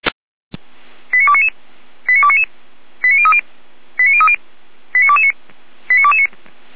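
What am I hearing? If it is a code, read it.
Five-tone selective call (ZVEI), heard as quick runs of five tones, tones of about 70 ms. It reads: 89299 89299 89030 89030 89229 89229
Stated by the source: ZVEI five-tone selective call received by SDR, recorded by Feuerwehr Orlamünde.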